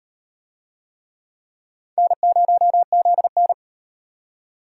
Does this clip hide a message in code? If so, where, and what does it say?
Yes: Morse code from 2.0 to 3.5 s, heard as a keyed tone, about 35 wpm, 700 Hz. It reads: N0ZN